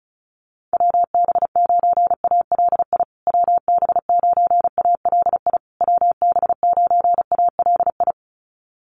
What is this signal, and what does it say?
Morse code, heard as a keyed tone, about 34 wpm, 700 Hz. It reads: WB9ALI WB9ALI WB9ALI
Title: USA Call Signs Repeated 3x 35wpm